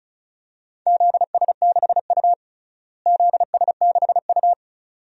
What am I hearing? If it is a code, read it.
Morse code, heard as a keyed tone, about 36 wpm, 700 Hz. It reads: ZS6U ZS6U
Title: DXCC-100 Call Signs - 4 Characters - Repeated 2x - 35wpm